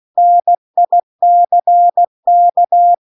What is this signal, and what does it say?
Morse code, heard as a keyed tone, about 16 wpm, 700 Hz. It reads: NICK
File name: CW-The-Big-Twoheaded-River-2-16WPM